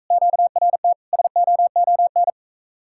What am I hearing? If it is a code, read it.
Morse code, heard as a keyed tone, about 44 wpm, 700 Hz. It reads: QRT SOON